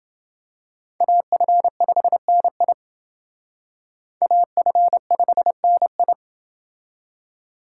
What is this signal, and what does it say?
Morse code, heard as a keyed tone, about 29 wpm, 700 Hz. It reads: AF5NI AF5NI